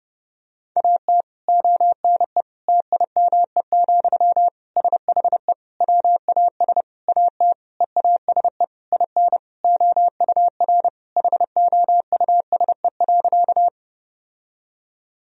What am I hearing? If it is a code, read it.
Morse code, heard as a keyed tone, about 30 wpm, 700 Hz. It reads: AT ONE TIME, SHE WAS AT EASE IN OUR HOUSE.
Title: Sentences from Top 1000 Words - Review Part 1 - 30wpm